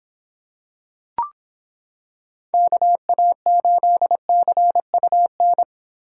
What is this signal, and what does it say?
Morse code, heard as a keyed tone, about 25 wpm, 700 Hz. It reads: KA8CUN